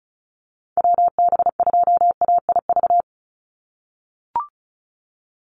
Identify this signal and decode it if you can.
Morse code, heard as a keyed tone, about 34 wpm, 700 Hz. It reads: WB2AIV
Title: Call Signs - 35wpm